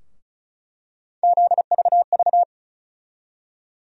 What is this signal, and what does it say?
Morse code, heard as a keyed tone, about 34 wpm, 700 Hz. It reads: ZVV
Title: USA Call Sign Postfixes - 3 Letters 35wpm